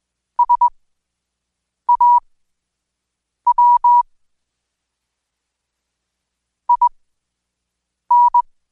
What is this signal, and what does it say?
Morse code, heard as a keyed tone, about 21 wpm, spaced as for 6 wpm, 970 Hz. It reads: SAW IN
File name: Code Quick/AudioDownld2/Seg109-Day14-DayDrill 14